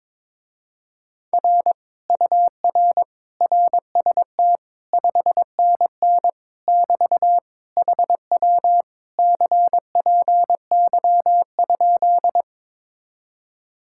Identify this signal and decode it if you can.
Morse code, heard as a keyed tone, about 22 wpm, 700 Hz. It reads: R UR RST 5NN <BT> HW CPY?
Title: Speed-Racing Common QSO Phrases Encore - 22wpm